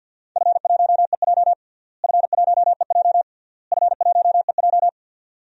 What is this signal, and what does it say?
Morse code, heard as a keyed tone, about 52 wpm, 700 Hz. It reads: F1EJ F1EJ F1EJ
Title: DXCC-100 Call Signs - 4 Characters - Repeated 3x - 50wpm